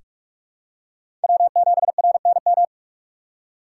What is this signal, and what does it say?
Morse code, heard as a keyed tone, about 46 wpm, 700 Hz. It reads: W7RNM
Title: USA Call Signs - 5 Characters - 6x Character Spacing - 45wpm